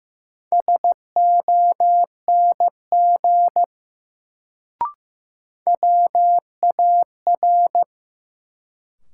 Morse code, heard as a keyed tone, about 15 wpm, 700 Hz. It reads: SONG WAR
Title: Sentences from Top 400 Words - New Word Drill - 15wpm